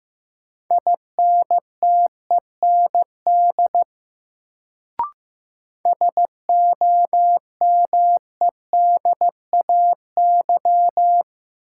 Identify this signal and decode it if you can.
Morse code, heard as a keyed tone, about 15 wpm, 700 Hz. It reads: INTEND SOMEDAY
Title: Sentences from Top 1600 Words - New Word Drill - 15wpm